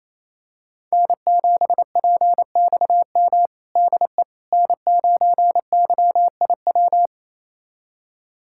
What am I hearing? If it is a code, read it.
Morse code, heard as a keyed tone, about 28 wpm, 700 Hz. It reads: N7PXM DE N9YIW
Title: USA Callsign DE Callsign - 5 Characters - 28wpm